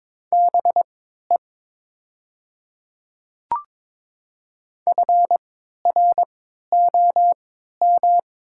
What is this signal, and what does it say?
Morse code, heard as a keyed tone, about 22 wpm, spaced as for 13 wpm, 700 Hz. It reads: BE FROM
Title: 3x Character Spacing - Top 300 Words - 22wpm